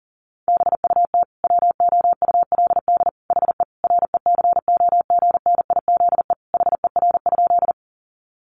Morse code, heard as a keyed tone, about 40 wpm, 700 Hz. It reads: BUT WOULD HE RECOGNIZE HER?
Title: Sentences from Top 1400 Words 40wpm